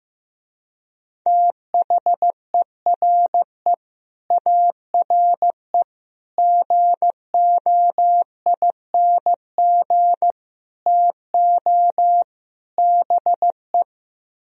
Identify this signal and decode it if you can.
Morse code, heard as a keyed tone, about 15 wpm, 700 Hz. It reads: THERE ARE GOING TO BE
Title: Speed-Racing - Sets of 5 Words 15wpm